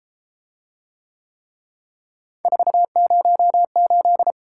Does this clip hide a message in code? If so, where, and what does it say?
Yes: Morse code from 2.5 to 4.3 s, heard as a keyed tone, about 32 wpm, 700 Hz. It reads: 408